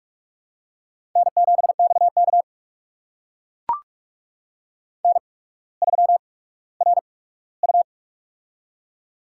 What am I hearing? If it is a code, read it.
Morse code, heard as a keyed tone, about 45 wpm, spaced as for 19 wpm, 700 Hz. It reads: N7XK N3RU
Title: USA Call Signs - 4 Characters - 8x Character Spacing - 45wpm